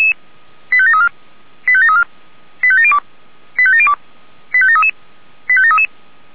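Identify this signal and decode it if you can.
Five-tone selective call (ZVEI), heard as quick runs of five tones, tones of about 70 ms. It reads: 85734 85734 85702 85702 85733 85733
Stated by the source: ZVEI five-tone selective call 85942 received via SDR.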